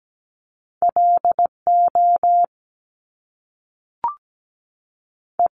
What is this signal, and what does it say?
Morse code, heard as a keyed tone, about 17 wpm, 700 Hz. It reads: LO E
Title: Top Two Letter Combinations in Words 17wpm